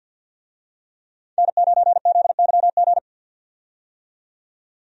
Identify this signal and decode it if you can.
Morse code, heard as a keyed tone, about 50 wpm, 700 Hz. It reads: N9ZYG
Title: USA Call Signs - 5 Characters - 50wpm